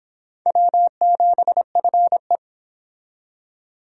Morse code, heard as a keyed tone, about 25 wpm, 700 Hz. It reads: W7FE